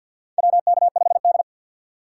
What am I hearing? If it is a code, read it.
Morse code, heard as a keyed tone, about 47 wpm, 700 Hz. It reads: WK5D
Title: USA Call Signs - 6x Character Spacing - 50wpm